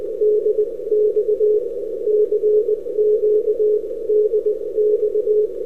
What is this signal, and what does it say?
Morse code, heard as a keyed tone, about 20 wpm, spaced as for 17 wpm, 420 Hz. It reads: DX CQ DX
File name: n6tr / ru3dx 11-Oct-2008